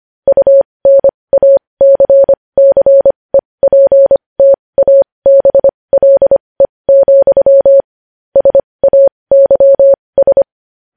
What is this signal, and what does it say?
Morse code, heard as a keyed tone, about 25 wpm, spaced as for 20 wpm, 550 Hz. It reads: UNACCEPTABLE, SAYS
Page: news headlines in morse code at 20 wpm